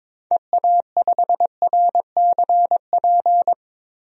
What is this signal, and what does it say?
Morse code, heard as a keyed tone, about 22 wpm, 700 Hz. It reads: EA5RCP
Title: DXCC-100 Call Signs - 6 Characters - Repeated 3x - 22wpm